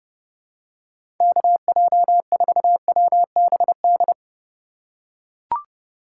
Morse code, heard as a keyed tone, about 30 wpm, 700 Hz. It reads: KJ4WBD